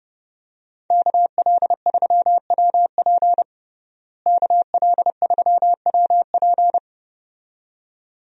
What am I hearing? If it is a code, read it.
Morse code, heard as a keyed tone, about 32 wpm, 700 Hz. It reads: KL3WP KL3WP
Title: USA Call Signs - 5 Characters - Repeated 2x - 30wpm